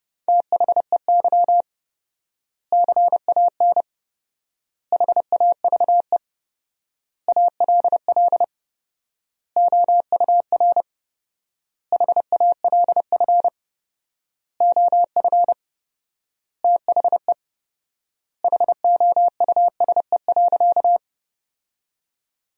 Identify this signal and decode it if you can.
Morse code, heard as a keyed tone, about 32 wpm, 700 Hz. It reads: THEY CAN HAVE ALL OUR HALF OF THE HOUSE.
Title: 4x Word-Spacing - Sentences from Top 400 Words - 30wpm